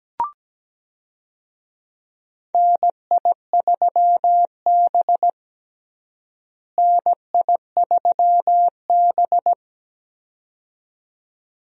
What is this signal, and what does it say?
Morse code, heard as a keyed tone, about 17 wpm, 700 Hz. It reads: NI3B NI3B
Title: USA Call Signs - 4 Characters - Repeated 2x - 17wpm